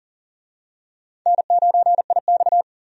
Morse code, heard as a keyed tone, about 40 wpm, 700 Hz. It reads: N9IX
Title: USA Call Signs - 4 Characters - 40wpm